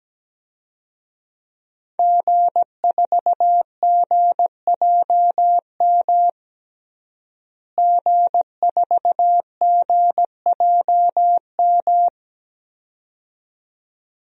Morse code, heard as a keyed tone, about 17 wpm, 700 Hz. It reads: G4GJM G4GJM
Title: DXCC-100 Call Signs - 5 Characters - Repeated 2x - 17wpm